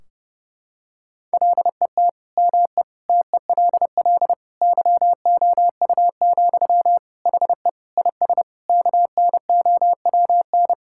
Morse code, heard as a keyed tone, about 32 wpm, 700 Hz. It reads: LET ME TELL YOU, HE IS KNOWN
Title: Sentences from Top 1600 Words 30wpm